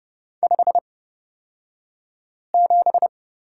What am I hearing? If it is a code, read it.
Morse code, heard as a keyed tone, about 30 wpm, spaced as for 5 wpm, 700 Hz. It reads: 57
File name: Lesson 37d - Letter x - Two Characters 30wpm